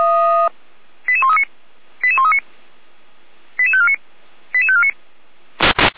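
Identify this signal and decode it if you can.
Five-tone selective call (ZVEI), heard as quick runs of five tones, tones of about 70 ms. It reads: 88139 88139 88549 88549
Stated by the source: ZVEI five-tone selective call 88531 received by SDR.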